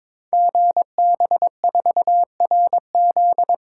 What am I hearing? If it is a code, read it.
Morse code, heard as a keyed tone, about 22 wpm, 700 Hz. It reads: GB4RZ
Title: Code Groups - 5 Letter:Numbers - 22wpm